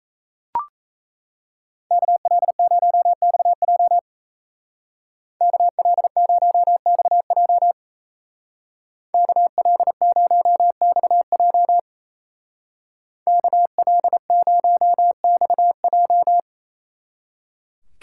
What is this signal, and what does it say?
Morse code, heard as a keyed tone, about 35 wpm, 700 Hz. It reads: KL0XJ KL0XJ KL0XJ KL0XJ
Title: Speed Racing - USA 2x2 Call Signs 28wpm